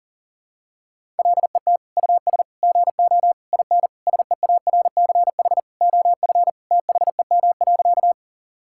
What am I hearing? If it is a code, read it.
Morse code, heard as a keyed tone, about 37 wpm, 700 Hz. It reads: LET US GO IN SEARCH OF THEM.
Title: Sentences from Top 1000 Words - Review Part 5 - 40wpm